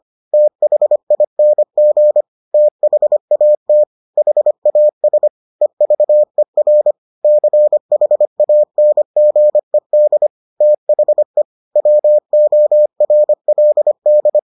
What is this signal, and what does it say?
Morse code, heard as a keyed tone, about 25 wpm, 600 Hz. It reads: THING THAT HAS EVER CHANGED THE WORLD